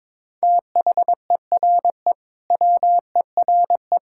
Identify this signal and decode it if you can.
Morse code, heard as a keyed tone, about 22 wpm, 700 Hz. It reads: THERE WERE
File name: practice / 22wpm / Sentences with 2-Word Phrases - 22wpm